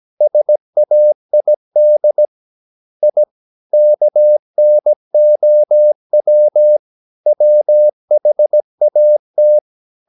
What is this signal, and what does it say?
Morse code, heard as a keyed tone, about 17 wpm, 600 Hz. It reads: SAID I KNOW WHAT
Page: Romanen Never af Ken Follet på engelsk.